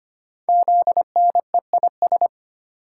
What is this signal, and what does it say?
Morse code, heard as a keyed tone, about 25 wpm, 700 Hz. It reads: ZNEIS
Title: Code Groups - 5 Letters - 25wpm